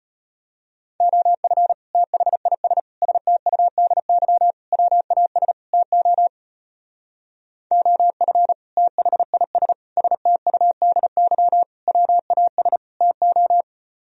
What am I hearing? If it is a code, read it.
Morse code, heard as a keyed tone, about 30 wpm, 700 Hz. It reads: OF THIS STUDY WAS TO OF THIS STUDY WAS TO
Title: Speed-Racing - Sets of 5 Words 25wpm